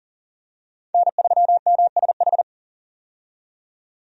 Morse code, heard as a keyed tone, about 40 wpm, 700 Hz. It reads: N3MSH
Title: USA Call Signs - 5 Characters - 40wpm